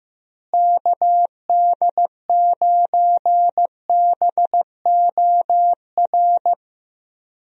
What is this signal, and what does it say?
Morse code, heard as a keyed tone, about 14 wpm, 700 Hz. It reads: KD9BOR
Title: USA Call Signs - 6 Characters - Repeated 3x - 15wpm